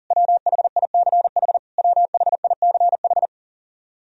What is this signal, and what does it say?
Morse code, heard as a keyed tone, about 40 wpm, 700 Hz. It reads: WHICH WHICH
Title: Top 100 Words Repeated 40wpm